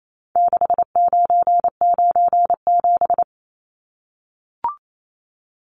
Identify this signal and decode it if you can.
Morse code, heard as a keyed tone, about 28 wpm, 700 Hz. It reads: 6997
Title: Numbers - 4-Digit 28wpm